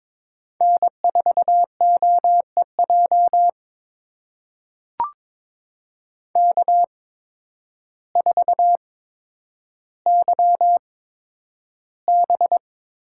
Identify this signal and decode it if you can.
Morse code, heard as a keyed tone, about 22 wpm, spaced as for 10 wpm, 700 Hz. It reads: N4OEJ K4YB
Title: USA Call Signs - 8x Character Spacing - 22wpm